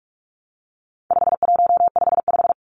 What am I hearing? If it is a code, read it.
Morse code, heard as a keyed tone, about 44 wpm, 700 Hz. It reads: 5155
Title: Numbers - 4-Digit 45wpm